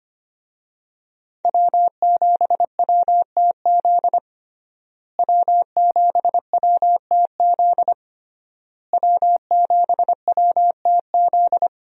Morse code, heard as a keyed tone, about 25 wpm, 700 Hz. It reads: W7WTZ W7WTZ W7WTZ
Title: USA Call Signs - 5 Characters - Repeated 3x - 25wpm